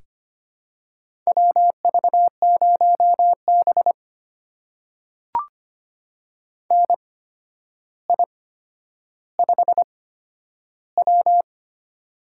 Morse code, heard as a keyed tone, about 25 wpm, spaced as for 11 wpm, 700 Hz. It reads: WV0B NI5W
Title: USA Call Signs - 8x Character Spacing - 25wpm